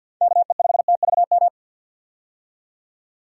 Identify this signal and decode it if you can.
Morse code, heard as a keyed tone, about 47 wpm, 700 Hz. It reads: KE5TVM
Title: USA Call Signs - 6 Characters - 50wpm